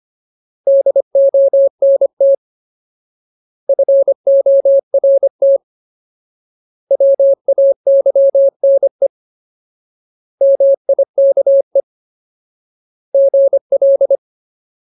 Morse code, heard as a keyed tone, about 25 wpm, 550 Hz. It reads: DONT FORT WAYNE MIKE GL
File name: Ultimate Word List / 25WPM Ultimate